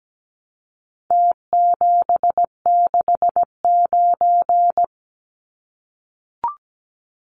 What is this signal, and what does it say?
Morse code, heard as a keyed tone, about 17 wpm, 700 Hz. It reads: T769
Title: Numbers - Contest Style - 4-Digit 17wpm